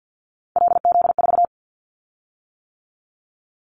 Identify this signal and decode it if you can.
Morse code, heard as a keyed tone, about 50 wpm, 700 Hz. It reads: LZ4
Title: Callsign Prefixes for Top 25 DXCC - 50wpm